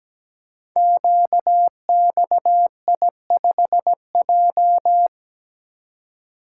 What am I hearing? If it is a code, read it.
Morse code, heard as a keyed tone, about 17 wpm, 700 Hz. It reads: QXI5J
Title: Code Groups - 5 Letter:Numbers - 17wpm